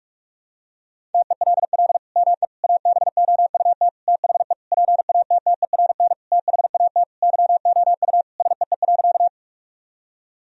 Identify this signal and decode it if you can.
Morse code, heard as a keyed tone, about 46 wpm, 700 Hz. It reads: TELL ME ABOUT THE PATTERN THAT YOU SEE.
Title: Sentences from Top 1000 Words - Review Part 2 - 45wpm